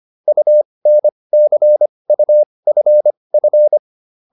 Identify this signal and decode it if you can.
Morse code, heard as a keyed tone, about 25 wpm, spaced as for 20 wpm, 600 Hz. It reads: UNCUFF
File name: UN6 20